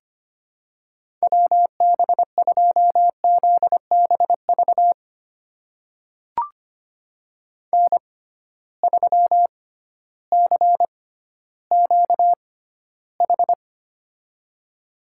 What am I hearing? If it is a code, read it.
Morse code, heard as a keyed tone, about 25 wpm, spaced as for 14 wpm, 700 Hz. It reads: WB2ZBV N3CQH
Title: USA Call Signs - 6x Character Spacing - 25wpm